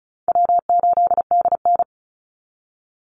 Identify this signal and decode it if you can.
Morse code, heard as a keyed tone, about 34 wpm, 700 Hz. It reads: W8DN
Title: Easy Call Signs Repeated 2x - 35wpm